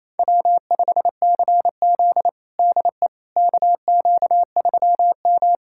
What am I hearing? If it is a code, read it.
Morse code, heard as a keyed tone, about 29 wpm, 700 Hz. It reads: W5CZ DE KQ3M
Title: USA Callsign DE Callsign - 4 Characters - 28wpm